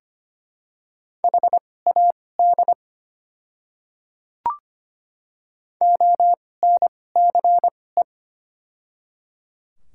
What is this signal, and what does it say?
Morse code, heard as a keyed tone, about 25 wpm, spaced as for 18 wpm, 700 Hz. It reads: HAD ONCE
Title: 2x Character Spacing - Top 300 Words - 25wpm